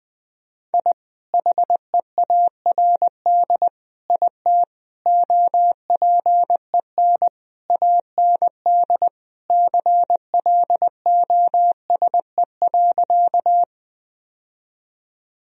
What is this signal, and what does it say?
Morse code, heard as a keyed tone, about 20 wpm, 700 Hz. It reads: I HEARD IT OPEN AND CLOSE.
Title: Sentences from Top 1000 Words - Review Part 1 - 20wpm